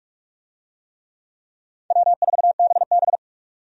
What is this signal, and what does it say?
Morse code, heard as a keyed tone, about 45 wpm, 700 Hz. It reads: W4BB
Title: USA Call Signs - 4 Characters - 45wpm